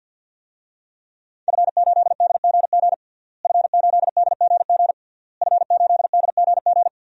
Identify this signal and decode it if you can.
Morse code, heard as a keyed tone, about 48 wpm, 700 Hz. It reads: F8DGG F8DGG F8DGG
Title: DXCC-100 Call Signs - 5 Characters - Repeated 3x - 50wpm